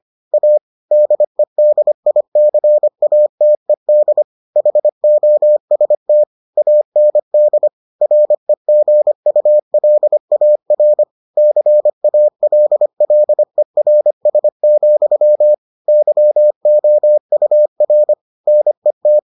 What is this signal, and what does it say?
Morse code, heard as a keyed tone, about 25 wpm, 600 Hz. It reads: A DEDICATED HOST AND REGULAR CALLERS, YOUR NET